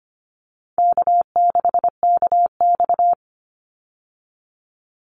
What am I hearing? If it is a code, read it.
Morse code, heard as a keyed tone, about 25 wpm, 700 Hz. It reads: K6KX
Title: Easy Call Signs Repeated 2x - 25wpm